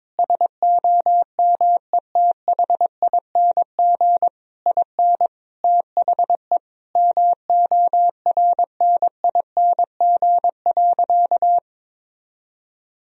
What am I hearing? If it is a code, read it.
Morse code, heard as a keyed tone, about 22 wpm, 700 Hz. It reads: SOMETHING IN THE MORNING.
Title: Sentences from Top 1000 Words - Review Part 4 - 22wpm